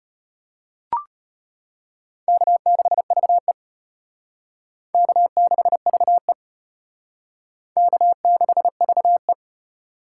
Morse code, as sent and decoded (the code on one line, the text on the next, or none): -.- -.... ...- . / -.- -.... ...- . / -.- -.... ...- .
K6VE K6VE K6VE